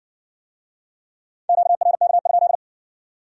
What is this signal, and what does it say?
Morse code, heard as a keyed tone, about 56 wpm, 700 Hz. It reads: QRK?